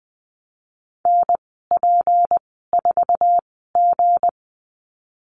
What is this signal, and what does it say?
Morse code, heard as a keyed tone, about 19 wpm, spaced as for 14 wpm, 700 Hz. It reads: NP4G
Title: 161 Lesson 26g - Letter g - Call Signs 20wpm